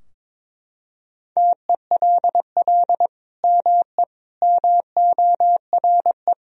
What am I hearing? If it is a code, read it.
Morse code, heard as a keyed tone, about 22 wpm, 700 Hz. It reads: TELL ME MORE
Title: Sentences from Top 1000 Words - Review Part 2 - 22wpm